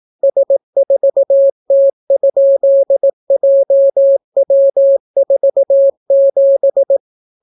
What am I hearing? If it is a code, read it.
Morse code, heard as a keyed tone, about 18 wpm, 550 Hz. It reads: S4T?JW47